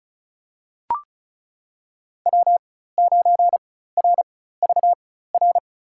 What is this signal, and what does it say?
Morse code, heard as a keyed tone, about 36 wpm, spaced as for 16 wpm, 700 Hz. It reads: W9RVR
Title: USA Call Signs - 5 Characters - 4x Character Spacing - 35wpm